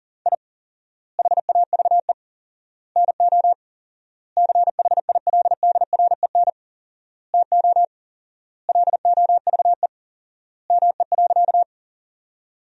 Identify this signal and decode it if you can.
Morse code, heard as a keyed tone, about 40 wpm, 700 Hz. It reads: I HAVE NO CHILDREN TO LOVE ME.
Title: 4x Word-Spacing - Sentences from Top 400 Words - 40wpm